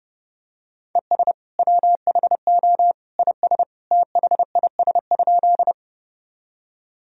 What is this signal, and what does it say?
Morse code, heard as a keyed tone, about 30 wpm, 700 Hz. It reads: ES WHO IS THIS?
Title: Sentences from Top 100 Words and Abbreviations 30wpm